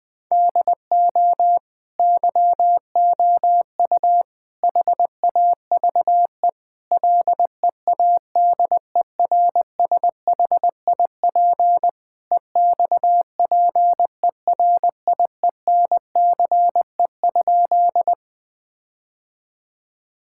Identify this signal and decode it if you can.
Morse code, heard as a keyed tone, about 20 wpm, 700 Hz. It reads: DO YOU HAVE LEADERSHIP EXPERIENCE?
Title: Sentences from Top 1600 Words 20wpm